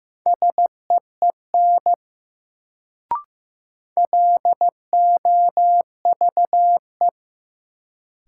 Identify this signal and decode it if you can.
Morse code, heard as a keyed tone, about 15 wpm, 700 Hz. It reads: SEEN LOVE
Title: Sets of 5 Words Encore - New Word Drill - 15wpm